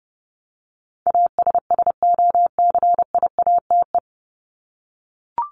Morse code, heard as a keyed tone, about 30 wpm, 700 Hz. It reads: ASSOCIATE
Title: Top 1751-2000 Words 30wpm